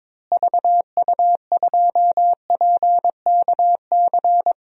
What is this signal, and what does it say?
Morse code, heard as a keyed tone, about 22 wpm, 700 Hz. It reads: VU2PKC
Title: DXCC-100 Call Signs - 6 Characters - Repeated 3x - 22wpm